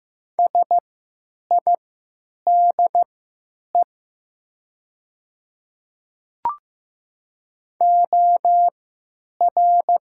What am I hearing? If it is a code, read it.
Morse code, heard as a keyed tone, about 15 wpm, spaced as for 9 wpm, 700 Hz. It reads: SIDE OR
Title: 3x Character Spacing - Top 100 Words - 15wpm